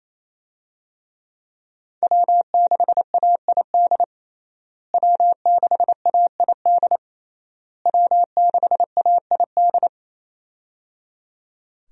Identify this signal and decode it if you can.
Morse code, heard as a keyed tone, about 28 wpm, 700 Hz. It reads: W6AID W6AID W6AID
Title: USA Call Signs - 5 Characters - Repeated 3x - 28wpm